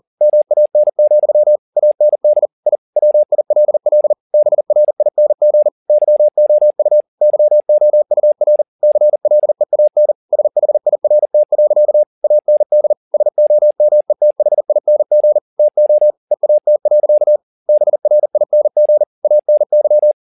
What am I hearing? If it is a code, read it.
Morse code, heard as a keyed tone, about 40 wpm, 600 Hz. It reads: MAN, AND I WILL BRING YOU YOUR CLEAN SHIRT. AND SOMETHING TO EAT. BRING ANY
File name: the old man and the sea 2 40wpm